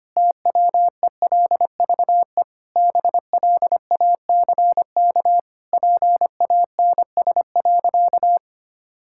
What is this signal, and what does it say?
Morse code, heard as a keyed tone, about 25 wpm, 700 Hz. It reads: TWELVE BLACK PANS.
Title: Pangram Challenge v2 - 25wpm